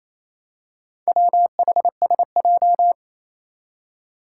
Morse code, heard as a keyed tone, about 28 wpm, 700 Hz. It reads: WHSJ
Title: Four Challenging Letter-Number Combinations 28wpm